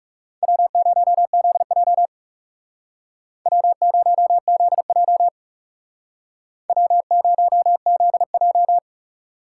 Morse code, heard as a keyed tone, about 36 wpm, 700 Hz. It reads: W0ZJ W0ZJ W0ZJ
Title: Speed Racing - USA 1x2 Call Signs 30wpm